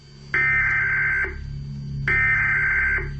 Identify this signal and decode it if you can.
Emergency Alert System (SAME) tones (header bursts), heard as harsh buzzing bursts. It is heard from 0.5 to 3.0 s.